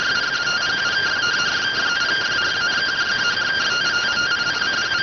Two-tone teleprinter signal, about 170 Hz shift.